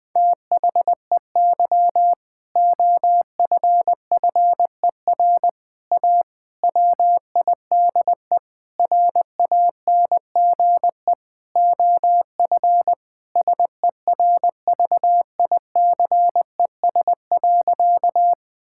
Morse code, heard as a keyed tone, about 20 wpm, 700 Hz. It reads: THEY OFFER A WIDE RANGE OF SERVICES.
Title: Sentences with 4-Word Phrases - Spoken Beginning - 20wpm